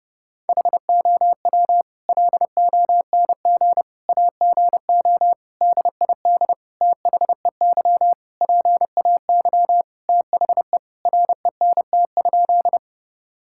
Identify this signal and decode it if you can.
Morse code, heard as a keyed tone, about 30 wpm, 700 Hz. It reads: HOW LONG AGO DID THEY PAY THE RENT?
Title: Sentences from Top 1000 Words - Review Part 5 - 30wpm